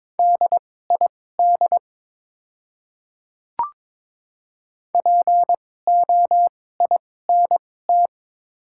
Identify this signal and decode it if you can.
Morse code, heard as a keyed tone, about 22 wpm, spaced as for 16 wpm, 700 Hz. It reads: DID POINT
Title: 2x Character Spacing - Top 200 Words - 22wpm